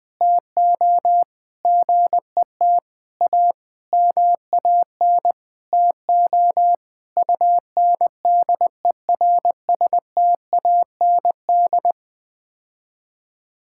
Morse code, heard as a keyed tone, about 20 wpm, 700 Hz. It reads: TO GET A MAN TO UNDERSTAND
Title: Sets of 9 Words - 20wpm